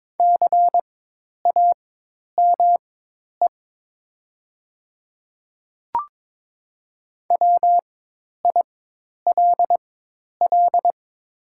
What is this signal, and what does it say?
Morse code, heard as a keyed tone, about 22 wpm, spaced as for 10 wpm, 700 Hz. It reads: CAME WILL
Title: 4x Character Spacing - Top 200 Words - 22wpm